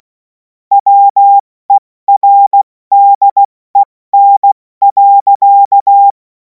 Morse code, heard as a keyed tone, about 16 wpm, spaced as for 14 wpm, 800 Hz. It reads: WERDEN.
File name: Morsekurs / Witze / Witz 03. März 2026 70 F80